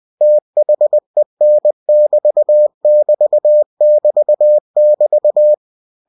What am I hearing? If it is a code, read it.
Morse code, heard as a keyed tone, about 20 wpm, 600 Hz. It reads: THEN <BT> <BT> <BT> <BT>
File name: cw-books / winnie-the-pooh / winnie-20wpm-page-0030